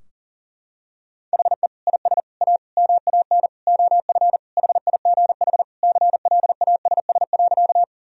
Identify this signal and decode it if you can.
Morse code, heard as a keyed tone, about 37 wpm, 700 Hz. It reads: HE IS A MAN OF HIGH CLASS.
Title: Sentences from Top 1000 Words - Review Part 2 - 40wpm